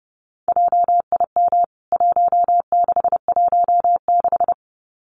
Morse code, heard as a keyed tone, about 30 wpm, 700 Hz. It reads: JIM 1616
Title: CWT Exchange 30wpm